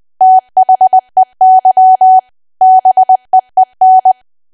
Morse code, heard as a keyed tone, about 20 wpm, 750 Hz. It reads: THEY BEEN
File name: cwa / W1AW-Level3 / 20WPM